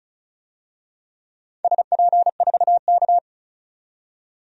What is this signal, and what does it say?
Morse code, heard as a keyed tone, about 36 wpm, 690 Hz. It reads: SP4K
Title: DXCC-100 Call Signs - 4 Characters - 35wpm